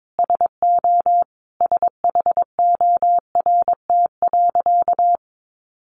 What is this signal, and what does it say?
Morse code, heard as a keyed tone, about 22 wpm, 700 Hz. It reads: SO SHORT.
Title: Sentences from Top 500 Words 22wpm